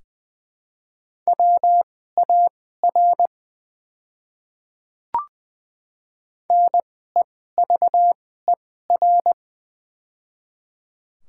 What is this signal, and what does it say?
Morse code, heard as a keyed tone, about 20 wpm, spaced as for 14 wpm, 700 Hz. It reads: WAR NEVER
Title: 2x Character Spacing - Top 500 Words - 20wpm